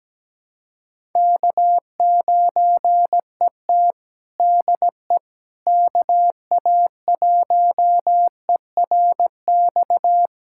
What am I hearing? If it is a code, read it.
Morse code, heard as a keyed tone, about 17 wpm, 700 Hz. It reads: K9ET DE KA1ERX
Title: USA Callsign DE Callsign 17wpm